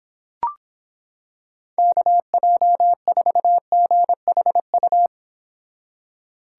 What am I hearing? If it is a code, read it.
Morse code, heard as a keyed tone, about 26 wpm, 700 Hz. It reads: KJ4GHU